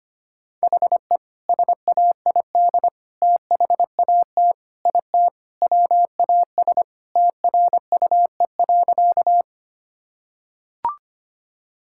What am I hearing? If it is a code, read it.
Morse code, heard as a keyed tone, about 25 wpm, 690 Hz. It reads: HE SAID THAT IT WAS TRUE.
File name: practice / 25wpm / Sentences with 2-Word Phrases - 25wpm